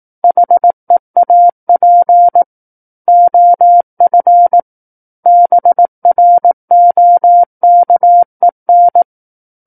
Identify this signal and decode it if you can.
Morse code, heard as a keyed tone, about 18 wpm, 700 Hz. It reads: HEAP OF BROKEN
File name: Book One / 09-Chapter 9 - 18 WPM